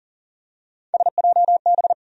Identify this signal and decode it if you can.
Morse code, heard as a keyed tone, about 40 wpm, 700 Hz. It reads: SJB
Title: USA Call Sign Postfixes - 3 Letters 40wpm